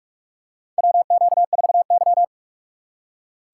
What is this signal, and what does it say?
Morse code, heard as a keyed tone, about 44 wpm, 700 Hz. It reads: WQ4Y
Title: USA Call Signs - 4 Characters - 3x Character Spacing - 45wpm